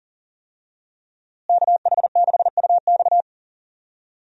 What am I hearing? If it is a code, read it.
Morse code, heard as a keyed tone, about 40 wpm, 700 Hz. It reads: KH6UX
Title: USA Call Signs - 5 Characters - Repeated 3x - 40wpm